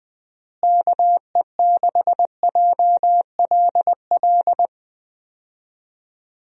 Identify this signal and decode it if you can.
Morse code, heard as a keyed tone, about 20 wpm, 700 Hz. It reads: KE6JLL
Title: USA Call Signs - 6 Characters - Repeated 3x - 20wpm